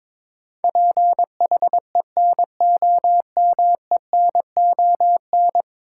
Morse code, heard as a keyed tone, about 22 wpm, 700 Hz. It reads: PHENOMENON
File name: practice / 22wpm / 10 Letter Words - First 3 Letters Given - 22wpm